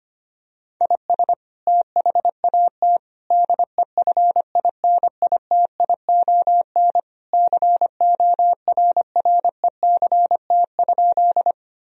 Morse code, heard as a keyed tone, about 25 wpm, 700 Hz. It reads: IS THAT DEFINITION CORRECT?
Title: Sentences from Top 1000 Words - Review Part 4 - 25wpm